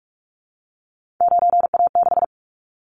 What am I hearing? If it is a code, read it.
Morse code, heard as a keyed tone, about 44 wpm, 700 Hz. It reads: 9A6
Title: Callsign Prefixes for Top 25 DXCC - Repeated 3x - 45wpm